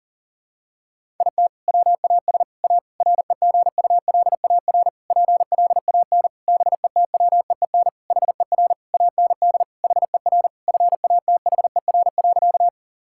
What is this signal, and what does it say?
Morse code, heard as a keyed tone, about 40 wpm, 700 Hz. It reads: IT WAS A REGULAR PLAN BETWEEN HER AND HER FATHER.
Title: Sentences from Top 1000 Words - Review Part 6 - 40wpm